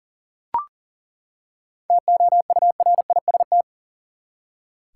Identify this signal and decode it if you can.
Morse code, heard as a keyed tone, about 39 wpm, 700 Hz. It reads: TOURIST
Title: Seven Letter Words - 40wpm